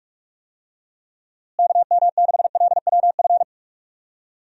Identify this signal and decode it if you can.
Morse code, heard as a keyed tone, about 46 wpm, 700 Hz. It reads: KM6LWF
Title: USA Call Signs - 6 Characters - Repeated 2x - 45wpm